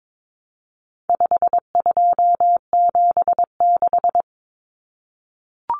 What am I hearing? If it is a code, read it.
Morse code, heard as a keyed tone, about 22 wpm, 700 Hz. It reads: 5276
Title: Numbers - 4-Digit 22wpm